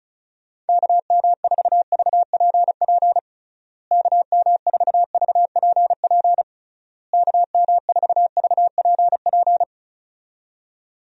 Morse code, heard as a keyed tone, about 34 wpm, 700 Hz. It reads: KM4VPP KM4VPP KM4VPP
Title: USA Call Signs - 6 Characters - Repeated 3x - 35wpm